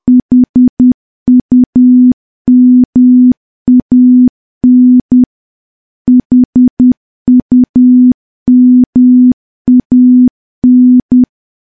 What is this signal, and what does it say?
Morse code, heard as a keyed tone, about 10 wpm, 250 Hz. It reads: HUMAN HUMAN
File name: morse / morse-training-10WPM